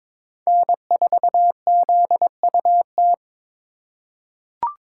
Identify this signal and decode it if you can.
Morse code, heard as a keyed tone, about 22 wpm, 700 Hz. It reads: N4ZUT
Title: USA Call Signs - 5 Characters - 3x Character Spacing - 22wpm